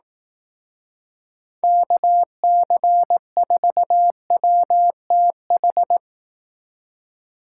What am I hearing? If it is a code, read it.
Morse code, heard as a keyed tone, about 18 wpm, 700 Hz. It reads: KC4WTH
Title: Speed Racing - USA 2x3 Call Signs 15wpm